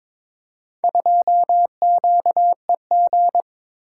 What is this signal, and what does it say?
Morse code, heard as a keyed tone, about 21 wpm, 700 Hz. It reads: 2QEG
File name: practice / 22wpm / Four Letter-Number Combinations - 22wpm